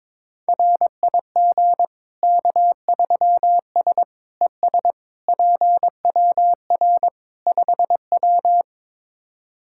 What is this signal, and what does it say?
Morse code, heard as a keyed tone, about 21 wpm, 700 Hz. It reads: RIG K3S ES PWR 5W
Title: K1USN SST QSO Practice 22wpm